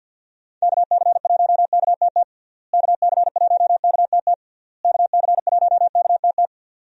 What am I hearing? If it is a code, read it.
Morse code, heard as a keyed tone, about 50 wpm, 700 Hz. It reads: KC1KTT KC1KTT KC1KTT
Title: USA Call Signs - 6 Characters - Repeated 3x - 50wpm